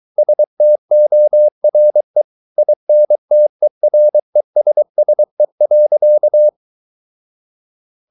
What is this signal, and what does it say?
Morse code, heard as a keyed tone, about 23 wpm, 600 Hz. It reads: STORE INTERESSE.
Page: Ferie 23 wpm